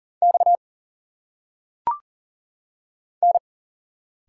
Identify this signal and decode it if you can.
Morse code, heard as a keyed tone, about 39 wpm, 700 Hz. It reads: X N
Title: Single Letters - Phonetic Pronunciation 40wpm